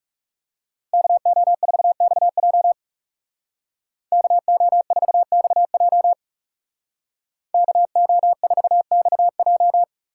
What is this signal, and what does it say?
Morse code, heard as a keyed tone, about 37 wpm, 700 Hz. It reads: KO4XJ KO4XJ KO4XJ
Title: Speed Racing - USA 2x2 Call Signs 30wpm